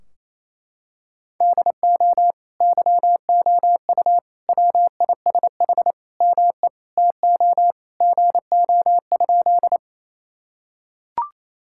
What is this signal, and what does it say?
Morse code, heard as a keyed tone, about 28 wpm, 700 Hz. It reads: DO YOU WISH ME TO GO?
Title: Sentences from Top 1600 Words 28wpm